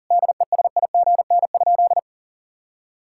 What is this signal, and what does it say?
Morse code, heard as a keyed tone, about 40 wpm, 700 Hz. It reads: DESIGN?
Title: Sentences from Top 1000 Words - Review Part 5 - 40wpm